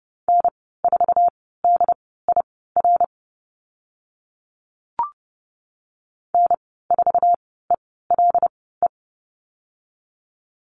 Morse code, heard as a keyed tone, about 30 wpm, spaced as for 17 wpm, 700 Hz. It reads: N4DIR N4ELE
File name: Lesson 15g - Number 5 - Call Signs 30wpm